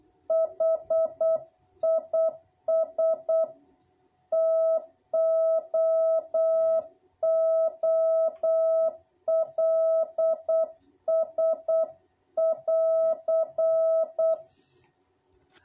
Morse code, heard as a keyed tone, about 8 wpm, 660 Hz. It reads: HIS TOOLS<AR>
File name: jh8cbh / 2021cw